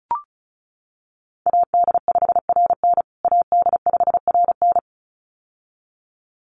Morse code, heard as a keyed tone, about 34 wpm, 700 Hz. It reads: AD5RN AD5RN
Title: USA Call Signs Repeated 2x 35wpm